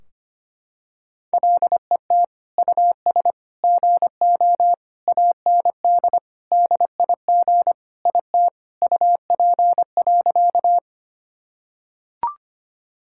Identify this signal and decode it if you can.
Morse code, heard as a keyed tone, about 25 wpm, 700 Hz. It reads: LET US GO AND DIG IT UP.
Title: Sentences from Top 1600 Words 25wpm